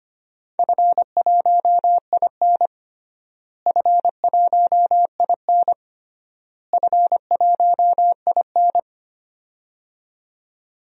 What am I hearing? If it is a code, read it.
Morse code, heard as a keyed tone, about 25 wpm, 700 Hz. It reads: F1IN F1IN F1IN
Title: DXCC-100 Call Signs - 4 Characters - Repeated 3x - 25wpm